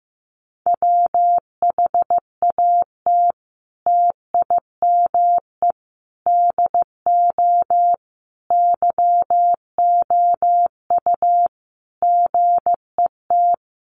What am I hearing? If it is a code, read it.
Morse code, heard as a keyed tone, about 15 wpm, 700 Hz. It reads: WHAT TIME DO YOU GET
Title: Sentences from Top 1500 Words 15wpm